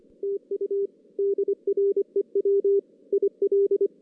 Morse code, heard as a keyed tone, about 24 wpm, 410 Hz. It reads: TU DREW IL
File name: 2016 JAN CW NAQP